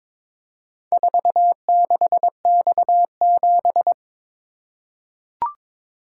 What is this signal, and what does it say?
Morse code, heard as a keyed tone, about 22 wpm, 690 Hz. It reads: 46X7
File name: practice / 22wpm / Four Letter-Number Combinations - 22wpm